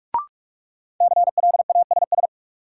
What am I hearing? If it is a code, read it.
Morse code, heard as a keyed tone, about 46 wpm, 700 Hz. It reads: CLASS
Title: Sentences from Top 1000 Words - Review Part 2 - New Word Drill - 45wpm